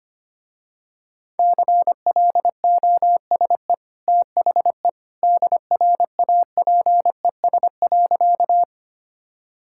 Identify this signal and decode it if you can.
Morse code, heard as a keyed tone, about 25 wpm, 700 Hz. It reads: CLOSE THE DRAPES.
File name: Easy Sentences with Hints - 25wpm